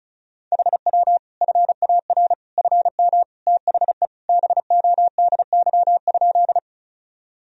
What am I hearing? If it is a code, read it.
Morse code, heard as a keyed tone, about 34 wpm, 700 Hz. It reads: HW FAR FM THE BODY?
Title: Sentences from Top 400 Words and Abbreviations 35wpm